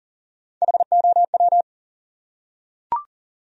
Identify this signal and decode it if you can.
Morse code, heard as a keyed tone, about 38 wpm, 700 Hz. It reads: HOW
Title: Top 100 Words Repeated 40wpm